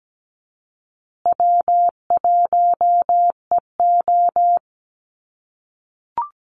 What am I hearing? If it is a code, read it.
Morse code, heard as a keyed tone, about 17 wpm, 700 Hz. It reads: W1EO
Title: USA Call Signs Repeated 3x 17wpm